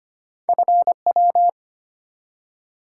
Morse code, heard as a keyed tone, about 25 wpm, 700 Hz. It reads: FW
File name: Two Letter-Number Combinations - 25wpm